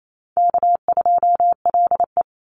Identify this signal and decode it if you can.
Morse code, heard as a keyed tone, about 28 wpm, 700 Hz. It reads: K2LE
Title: Easy Call Signs Repeated 3x - 28wpm